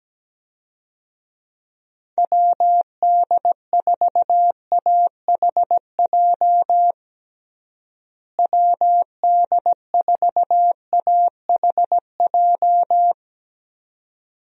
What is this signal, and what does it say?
Morse code, heard as a keyed tone, about 17 wpm, 700 Hz. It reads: WD4AHJ WD4AHJ
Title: USA Call Signs - 6 Characters - Repeated 2x - 17wpm